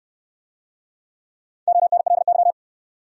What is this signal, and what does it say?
Morse code, heard as a keyed tone, about 68 wpm, 700 Hz. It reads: ONLY